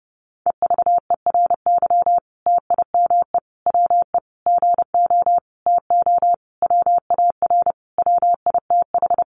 Morse code, heard as a keyed tone, about 28 wpm, 700 Hz. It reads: EVERY TIME WE GO TO WAR WITH
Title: Sentences from Top 1300 Words 30wpm